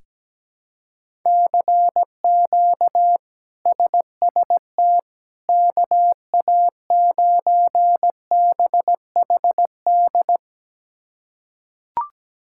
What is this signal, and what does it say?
Morse code, heard as a keyed tone, about 17 wpm, 700 Hz. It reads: CQ SST KA9BHD
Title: K1USN SST QSO Practice - 17wpm